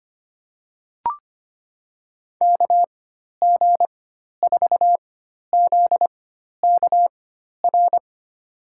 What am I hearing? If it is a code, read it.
Morse code, heard as a keyed tone, about 25 wpm, spaced as for 12 wpm, 700 Hz. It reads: KG4ZKR
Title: USA Call Signs - 6 Characters - 4x Character Spacing - 25wpm